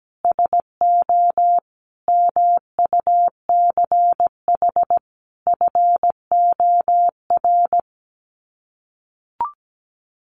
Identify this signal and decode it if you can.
Morse code, heard as a keyed tone, about 17 wpm, 700 Hz. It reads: SO MUCH FOR
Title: Sets of 5 Words 17wpm